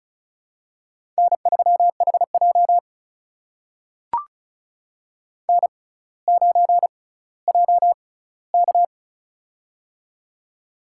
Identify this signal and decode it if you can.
Morse code, heard as a keyed tone, about 36 wpm, spaced as for 18 wpm, 700 Hz. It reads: N3HJ N9JK
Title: USA Call Signs - 4 Characters - 6x Character Spacing - 35wpm